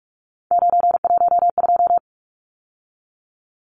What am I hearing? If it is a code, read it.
Morse code, heard as a keyed tone, about 44 wpm, 700 Hz. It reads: 912
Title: Numbers - 3-Digit 45wpm